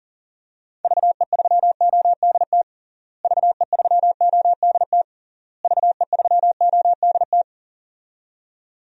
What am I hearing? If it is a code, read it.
Morse code, heard as a keyed tone, about 37 wpm, 700 Hz. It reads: VE3ODT VE3ODT VE3ODT
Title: DXCC-100 Call Signs - 6 Characters - Repeated 3x - 40wpm